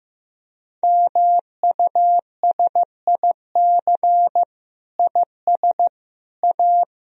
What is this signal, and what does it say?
Morse code, heard as a keyed tone, about 16 wpm, 700 Hz. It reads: MUSIC IS A
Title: Sentences from Top 300 Words and Abbreviations 15wpm